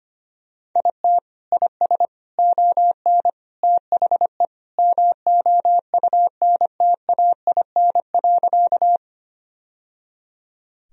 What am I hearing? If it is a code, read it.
Morse code, heard as a keyed tone, about 25 wpm, 700 Hz. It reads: IT IS ON THE MOUNTAIN.
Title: Sentences from Top 1000 Words - Review Part 1 - 25wpm